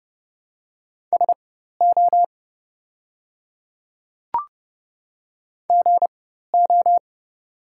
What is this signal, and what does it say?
Morse code, heard as a keyed tone, about 30 wpm, spaced as for 14 wpm, 700 Hz. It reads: SO GO
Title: 4x Character Spacing - Top 100 Words - 30wpm